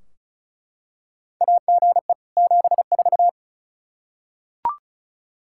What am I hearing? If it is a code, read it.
Morse code, heard as a keyed tone, about 36 wpm, 700 Hz. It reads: AGE 74
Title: K1USN SST QSO Practice 35wpm